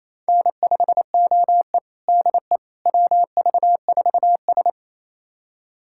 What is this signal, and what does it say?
Morse code, heard as a keyed tone, about 28 wpm, 700 Hz. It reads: N5OE DE WV4S
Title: USA Callsign DE Callsign - 4 Characters - 28wpm